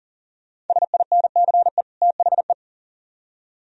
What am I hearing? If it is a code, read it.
Morse code, heard as a keyed tone, about 51 wpm, 690 Hz. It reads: SINCE THE